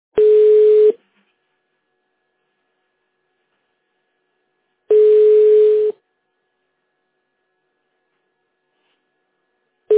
Ringback tone (Europe, 425 Hz).
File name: 1015. -8913-45fa-b45c